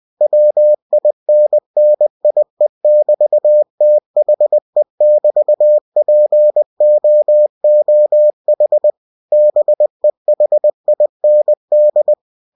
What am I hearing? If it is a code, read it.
Morse code, heard as a keyed tone, about 20 wpm, 600 Hz. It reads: WINNIE<BT>THE<BT>POOH BEHIND
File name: cw-books / winnie-the-pooh / winnie-20wpm-page-0111